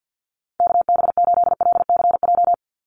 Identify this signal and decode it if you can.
Morse code, heard as a keyed tone, about 50 wpm, 700 Hz. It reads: KB8LCJ
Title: USA Call Signs 50wpm